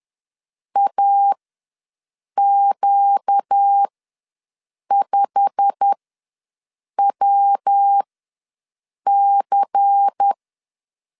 Morse code, heard as a keyed tone, about 10 wpm, spaced as for 6 wpm, 800 Hz. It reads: AQ5WC
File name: LEZIONI / 8 YJ45 / 8 YJ45 ws 10wpm 3843